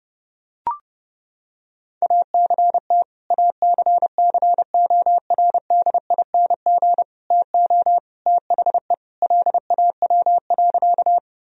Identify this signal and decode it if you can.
Morse code, heard as a keyed tone, about 30 wpm, 700 Hz. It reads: ACT ACCORDING TO THE LAW.
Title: Sentences from Top 1000 Words - Review Part 4 - 30wpm